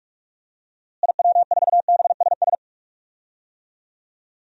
Morse code, heard as a keyed tone, about 43 wpm, 700 Hz. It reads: IW4BSS
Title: DXCC-100 Call Signs - 6 Characters - Repeated 3x - 45wpm